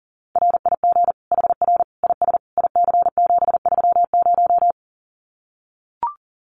Morse code, heard as a keyed tone, about 40 wpm, 700 Hz. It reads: RIG HR IS IC730
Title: Common QSO Phrases 40wpm